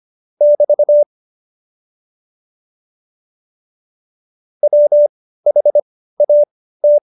Morse code, heard as a keyed tone, about 25 wpm, spaced as for 15 wpm, 590 Hz. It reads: <BT> WHAT
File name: Jokes-25-char-15-eff